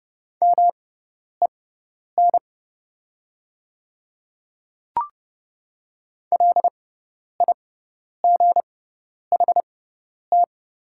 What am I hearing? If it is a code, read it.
Morse code, heard as a keyed tone, about 30 wpm, spaced as for 10 wpm, 700 Hz. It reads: MEN LIGHT
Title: 6x Character Spacing - Top 300 Words - 30wpm